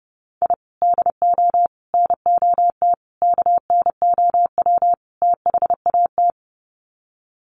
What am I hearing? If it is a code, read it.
Morse code, heard as a keyed tone, about 30 wpm, 700 Hz. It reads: I DO NOT KNOW THAT
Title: Sets of 5 Words 30wpm